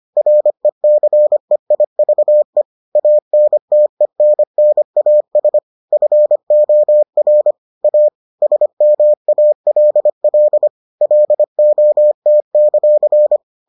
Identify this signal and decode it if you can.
Morse code, heard as a keyed tone, about 25 wpm, 600 Hz. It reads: RECEIVE ANTENNAS FOR A SMALL LOT;